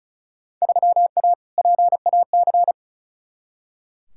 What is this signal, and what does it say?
Morse code, heard as a keyed tone, about 36 wpm, 700 Hz. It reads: 3A PAC
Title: Field Day Exchanges Encore - 35wpm